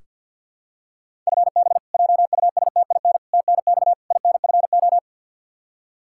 Morse code, heard as a keyed tone, about 47 wpm, 700 Hz. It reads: FB JUSTIN TNX INFO
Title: Practice QSO - Exchange 3 - More Info - 50wpm